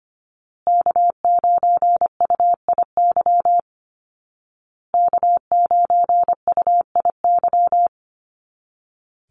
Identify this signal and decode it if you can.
Morse code, heard as a keyed tone, about 25 wpm, 700 Hz. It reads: K9UIY K9UIY
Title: Call Signs Repeated 2x - 25wpm